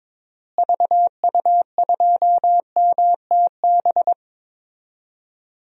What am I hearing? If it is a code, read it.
Morse code, heard as a keyed tone, about 22 wpm, 700 Hz. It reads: VU2MTB
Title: DXCC-100 Call Signs - 6 Characters - Repeated 3x - 22wpm